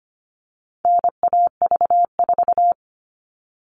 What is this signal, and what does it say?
Morse code, heard as a keyed tone, about 25 wpm, 700 Hz. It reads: NAV4
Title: All Cut Numbers - 4-Digit 25wpm